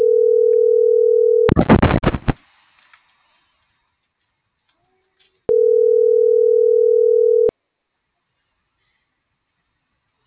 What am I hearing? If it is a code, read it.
Ringback tone (North America), 440 + 480 Hz.